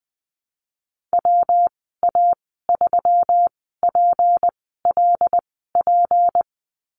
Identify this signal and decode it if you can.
Morse code, heard as a keyed tone, about 20 wpm, spaced as for 14 wpm, 700 Hz. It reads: WA3PLP
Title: 161 Lesson 26g - Letter g - Call Signs 20wpm